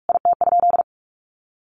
Morse code, heard as a keyed tone, about 44 wpm, 700 Hz. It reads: IT?
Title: Sentences from Top 1100 Words 45wpm